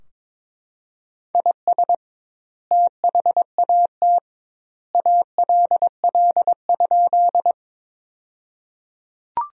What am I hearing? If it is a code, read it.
Morse code, heard as a keyed tone, about 22 wpm, 700 Hz. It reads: IS THAT ALL?
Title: 2x Word-Spacing - Sentences from Top 100 Words - 22wpm